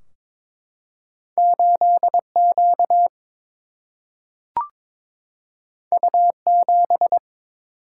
Morse code, heard as a keyed tone, about 22 wpm, 700 Hz. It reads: 8 Q U 7
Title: Single-Letter Single-Number 22wpm